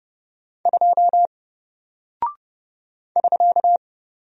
Morse code, heard as a keyed tone, about 30 wpm, 700 Hz. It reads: 2 <SK>
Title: Single Character (Letter:Number:Punctuation:Prosign) Alt 30wpm